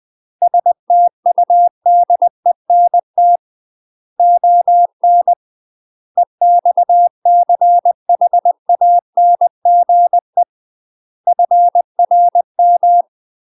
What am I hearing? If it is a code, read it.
Morse code, heard as a keyed tone, about 20 wpm, 700 Hz. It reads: STUDENT ON EXCHANGE FRM
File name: UK MASTER QSOs TRAINING / 20WPM / UK MASTER QSOs 0062